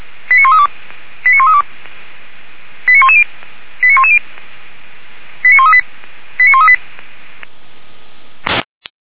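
Five-tone selective call (ZVEI), heard as quick runs of five tones, tones of about 70 ms. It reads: 89132 89132 89110 89110 89138 89138
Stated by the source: ZVEI five-tone call received on SDR, logged by Feuerwehr Orlamünde.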